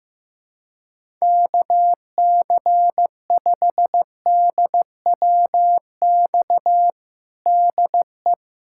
Morse code, heard as a keyed tone, about 15 wpm, 700 Hz. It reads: KC5DWX DE
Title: USA Callsign DE Callsign - 6 Characters - 15wpm